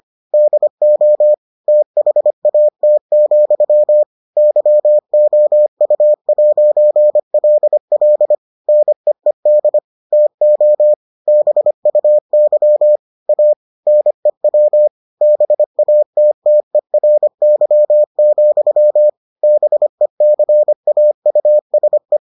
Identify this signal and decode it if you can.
Morse code, heard as a keyed tone, about 25 wpm, 600 Hz. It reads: DO THAT, YOU'LL NEED TO BUY A NEW BATTERY, BECAUSE